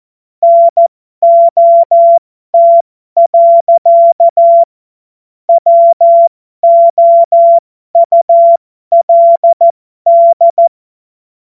Morse code, heard as a keyed tone, about 14 wpm, spaced as for 12 wpm, 680 Hz. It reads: NOT. WOULD
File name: Morsekurs / Witze